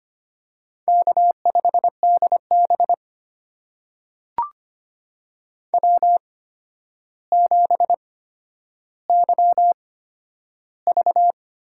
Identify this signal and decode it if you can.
Morse code, heard as a keyed tone, about 25 wpm, spaced as for 11 wpm, 700 Hz. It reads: K5DB W7YV
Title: USA Call Signs - 8x Character Spacing - 25wpm